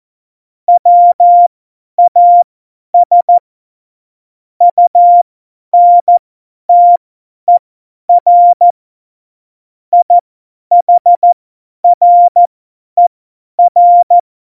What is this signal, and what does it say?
Morse code, heard as a keyed tone, about 14 wpm, spaced as for 10 wpm, 700 Hz. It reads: WAS UNTER IHRER